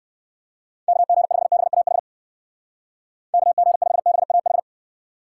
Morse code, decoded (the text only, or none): KG5BAH KG5BAH